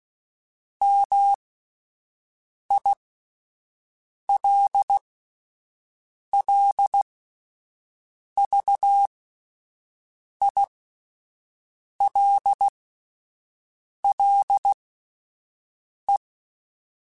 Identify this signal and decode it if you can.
Morse code, heard as a keyed tone, about 16 wpm, spaced as for 5 wpm, 800 Hz. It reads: MILLVILLE